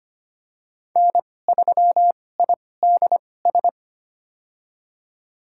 Morse code, heard as a keyed tone, about 25 wpm, spaced as for 18 wpm, 700 Hz. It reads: N3IDS
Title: USA Call Signs - 2x Character Spacing - 25wpm